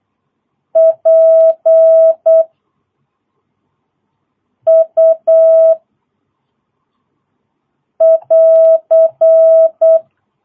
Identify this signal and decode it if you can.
Morse code, heard as a keyed tone, about 8 wpm, 660 Hz. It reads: P U <AR>
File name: jh8cbh / 2022cw